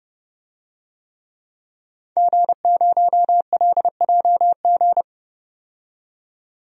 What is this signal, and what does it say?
Morse code, heard as a keyed tone, about 30 wpm, 700 Hz. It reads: G0LJG
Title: DXCC-100 Call Signs - 5 Characters - 30wpm